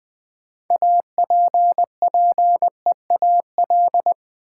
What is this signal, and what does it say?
Morse code, heard as a keyed tone, about 20 wpm, 700 Hz. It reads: APPEAL
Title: Sentences from Top 1300 Words - New Word Drill - 20wpm